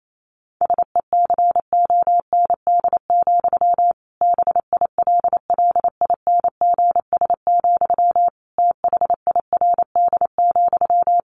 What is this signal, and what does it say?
Morse code, heard as a keyed tone, about 28 wpm, 700 Hz. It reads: SECOND, BILLINGS, THIRD,
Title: A Sound Of Thunder 28wpm